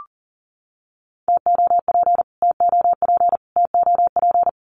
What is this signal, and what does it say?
Morse code, heard as a keyed tone, about 40 wpm, 700 Hz. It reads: TOP TOP TOP
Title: Top 500 Words Repeated 40wpm